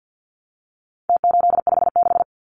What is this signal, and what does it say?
Morse code, heard as a keyed tone, about 49 wpm, 700 Hz. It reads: T856
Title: Numbers - Contest Style - 4-Digit 50wpm